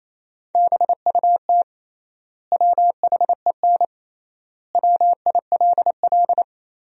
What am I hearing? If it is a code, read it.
Morse code, heard as a keyed tone, about 28 wpm, 700 Hz. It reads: BUT WHEN WILL
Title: 3x Word-Spacing - Sentences from Top 100 Words - 28wpm